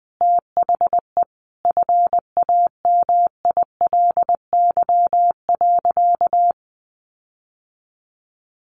Morse code, heard as a keyed tone, about 20 wpm, 700 Hz. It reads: THE FAMILY.
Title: Sentences from Top 1500 Words 20wpm